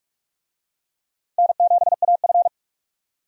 Morse code, heard as a keyed tone, about 44 wpm, 700 Hz. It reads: N7AF
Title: USA Call Signs - 4 Characters - 45wpm